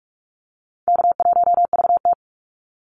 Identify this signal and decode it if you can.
Morse code, heard as a keyed tone, about 44 wpm, 700 Hz. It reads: K1VT
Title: Easy Call Signs Repeated 2x - 45wpm